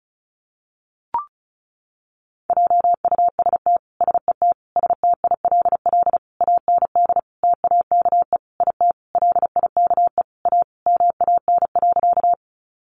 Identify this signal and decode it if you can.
Morse code, heard as a keyed tone, about 34 wpm, 700 Hz. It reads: JUST SET STILL AND TAKE IT LIKE A MAN.
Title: Sentences from Top 300 Words 35wpm